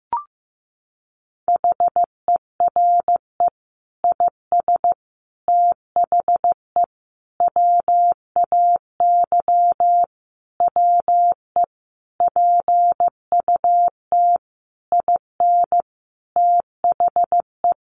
Morse code, heard as a keyed tone, about 14 wpm, 700 Hz. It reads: HERE IS THE WAY WE PUT IN THE
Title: Sentences from Top 200 Words 15wpm